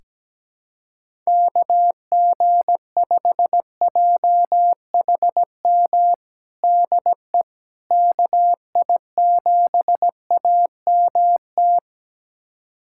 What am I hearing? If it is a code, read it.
Morse code, heard as a keyed tone, about 17 wpm, 700 Hz. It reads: KG5JHM DE KI7AMT
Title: USA Callsign DE Callsign - 6 Characters - 17wpm